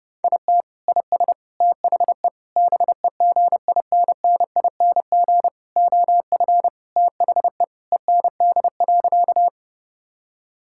Morse code, heard as a keyed tone, about 30 wpm, 700 Hz. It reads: IT IS THE BEGINNING OF THE END.
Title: Sentences from Top 1000 Words - Review Part 5 - 30wpm